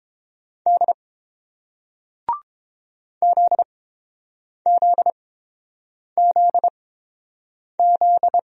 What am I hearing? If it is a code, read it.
Morse code, heard as a keyed tone, about 30 wpm, spaced as for 7 wpm, 690 Hz. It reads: D ZZZZ